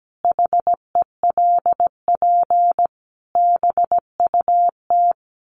Morse code, heard as a keyed tone, about 17 wpm, 700 Hz. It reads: HELP BUT
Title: Sets of 5 Words Encore2 17wpm